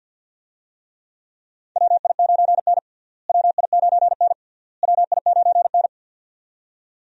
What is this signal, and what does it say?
Morse code, heard as a keyed tone, about 52 wpm, 700 Hz. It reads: WI9N WI9N WI9N
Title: USA Call Signs - 4 Characters - Repeated 3x - 50wpm